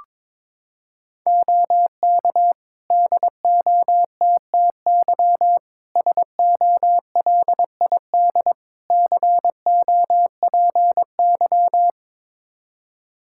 Morse code, heard as a keyed tone, about 22 wpm, 700 Hz. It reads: OK DOTTY SOLID COPY
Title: K1USN SST QSO Practice 22wpm